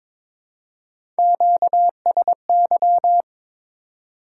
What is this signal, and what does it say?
Morse code, heard as a keyed tone, about 21 wpm, 700 Hz. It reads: QSY
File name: Top 90 QSO Elements - 8x Character Spacing - 22wpm